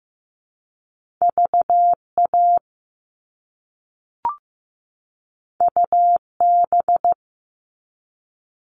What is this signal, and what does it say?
Morse code, heard as a keyed tone, about 15 wpm, 700 Hz. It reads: VA UB
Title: All Cut Numbers - 2-Digit 15wpm